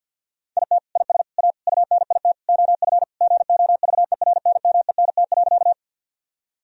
Morse code, heard as a keyed tone, about 50 wpm, 700 Hz. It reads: IT IS A UNIT OF GOVERNMENT.
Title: Sentences from Top 1000 Words - Review Part 2 - 50wpm